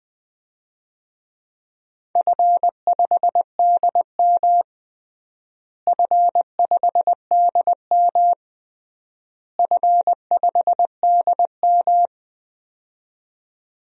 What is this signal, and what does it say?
Morse code, heard as a keyed tone, about 20 wpm, 700 Hz. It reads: F5DM F5DM F5DM
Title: DXCC-100 Call Signs - 4 Characters - Repeated 3x - 20wpm